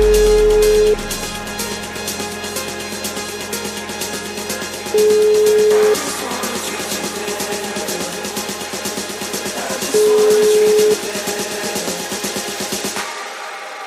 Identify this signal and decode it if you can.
Ringback tone (Europe, 425 Hz), with noise.